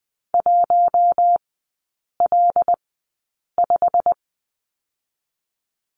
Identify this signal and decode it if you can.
Morse code, heard as a keyed tone, about 20 wpm, spaced as for 8 wpm, 690 Hz. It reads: 1L5
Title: 087 Lesson 15e - Number 5 - Three Characters 20wpm